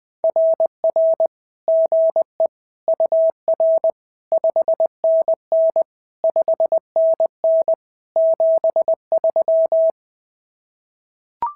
Morse code, heard as a keyed tone, about 20 wpm, 650 Hz. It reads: RR GE UR 5NN 5NN 73
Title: SOTA QSO Practice 20wpm